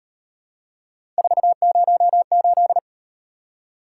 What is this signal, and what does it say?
Morse code, heard as a keyed tone, about 38 wpm, 700 Hz. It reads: 408